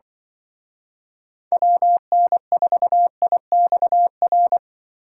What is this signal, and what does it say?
Morse code, heard as a keyed tone, about 25 wpm, 700 Hz. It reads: WN4IXR